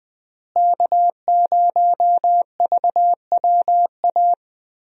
Morse code, heard as a keyed tone, about 20 wpm, 700 Hz. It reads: K0VWA
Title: USA Call Signs - 5 Characters - Repeated 3x - 20wpm